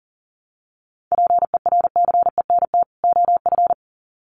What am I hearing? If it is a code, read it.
Morse code, heard as a keyed tone, about 39 wpm, 700 Hz. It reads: PERCENT OF
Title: Sets of 2 Words 40wpm